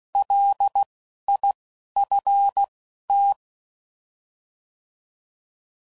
Morse code, heard as a keyed tone, about 16 wpm, spaced as for 11 wpm, 800 Hz. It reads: LIFT